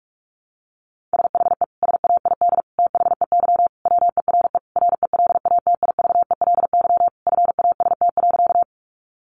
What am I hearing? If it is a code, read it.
Morse code, heard as a keyed tone, about 44 wpm, 700 Hz. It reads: SHE SAID THEY WERE RELATIVELY FAST.
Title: Sentences from Top 1100 Words 45wpm